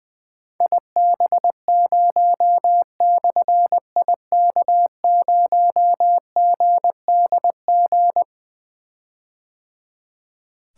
Morse code, heard as a keyed tone, about 20 wpm, 700 Hz. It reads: IB0/IK0GDG
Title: Difficult Call Signs 20wpm